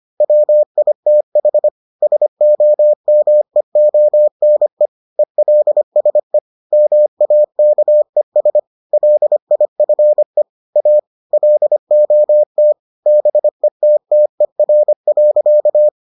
Morse code, heard as a keyed tone, about 25 wpm, 600 Hz. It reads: WITH SOMEONE ELSE MAKES LIFE A LOT BETTER.